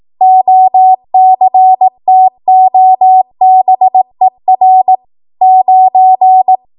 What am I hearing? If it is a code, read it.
Morse code, heard as a keyed tone, about 18 wpm, 750 Hz. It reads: OCTOBER 9
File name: Archive / 18 WPM / 18WPM